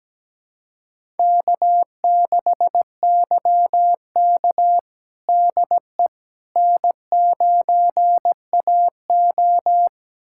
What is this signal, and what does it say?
Morse code, heard as a keyed tone, about 17 wpm, 700 Hz. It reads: K6YK DE N9AO
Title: USA Callsign DE Callsign - 4 Characters - 17wpm